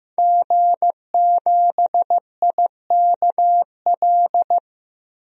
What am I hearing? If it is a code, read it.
Morse code, heard as a keyed tone, about 15 wpm, 700 Hz. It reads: G7IKL
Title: DXCC-100 Call Signs - 5 Characters - Repeated 2x - 15wpm